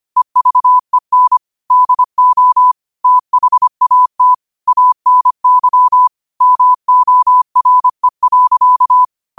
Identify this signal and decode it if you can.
Morse code, heard as a keyed tone, about 25 wpm, 1000 Hz. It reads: EVEN DO THAT ANY MORE.
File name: qotdfiles / 25wpm